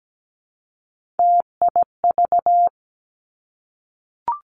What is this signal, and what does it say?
Morse code, heard as a keyed tone, about 17 wpm, 700 Hz. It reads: TIV
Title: Three Challenging Letter-Number Combinations 17wpm